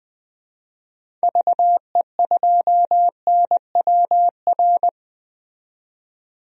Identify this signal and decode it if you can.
Morse code, heard as a keyed tone, about 21 wpm, 700 Hz. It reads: VE2NWR
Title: DXCC-100 Call Signs - 6 Characters - 20wpm